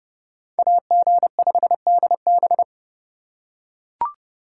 Morse code, heard as a keyed tone, about 31 wpm, 700 Hz. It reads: AG5DB